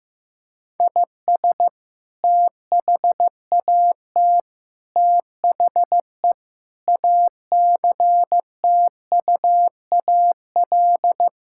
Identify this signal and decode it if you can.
Morse code, heard as a keyed tone, about 16 wpm, 700 Hz. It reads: IS THAT THE ACTUAL
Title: Sentences from Top 1000 Words - Review Part 5 - 15wpm